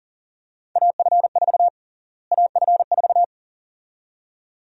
Morse code, heard as a keyed tone, about 40 wpm, 700 Hz. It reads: AF4 AF4
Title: USA Call Sign Prefixes - Repeated 2x - 40wpm